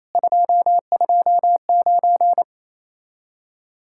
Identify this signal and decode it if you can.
Morse code, heard as a keyed tone, about 34 wpm, 700 Hz. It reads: 229
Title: Speed-Racing Common QSO Phrases - 28wpm